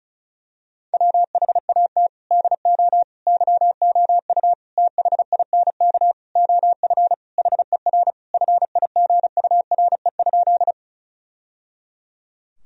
Morse code, heard as a keyed tone, about 36 wpm, 700 Hz. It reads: WHAT DO YOU THINK OF HER FIGURE?
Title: Sentences from Top 1000 Words - Review Part 2 - 35wpm